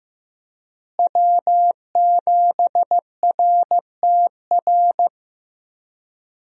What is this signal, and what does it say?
Morse code, heard as a keyed tone, about 15 wpm, 700 Hz. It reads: W7RTR
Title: USA Call Signs - 5 Characters - Repeated 3x - 15wpm